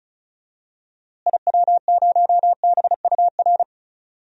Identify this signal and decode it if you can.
Morse code, heard as a keyed tone, about 36 wpm, 700 Hz. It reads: IW0BUR
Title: DXCC-100 Call Signs - 6 Characters - Repeated 3x - 35wpm